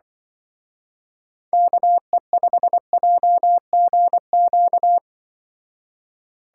Morse code, heard as a keyed tone, about 23 wpm, 700 Hz. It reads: KE5JGQ
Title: Speed Racing - USA 2x3 Call Signs 20wpm